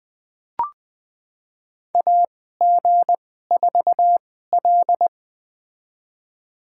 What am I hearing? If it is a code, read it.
Morse code, heard as a keyed tone, about 20 wpm, spaced as for 14 wpm, 700 Hz. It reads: AG4L
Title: USA Call Signs - 4 Characters - 2x Character Spacing - 20wpm